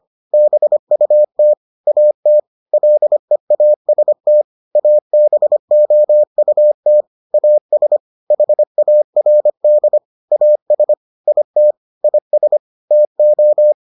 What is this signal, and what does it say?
Morse code, heard as a keyed tone, about 25 wpm, 600 Hz. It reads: BUT AT LEAST ABOUT AS HARD AS IT IS TO